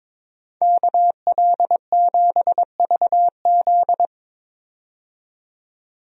Morse code, heard as a keyed tone, about 22 wpm, 700 Hz. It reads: KL7VZ
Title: DXCC-100 Call Signs - 5 Characters - 22wpm